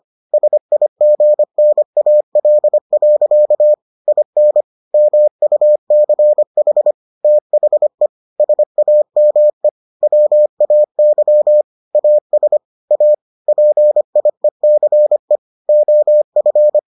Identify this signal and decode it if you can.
Morse code, heard as a keyed tone, about 25 wpm, 600 Hz. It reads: SIGNAL. IN MUCH THE SAME WAY AS A PIECE OF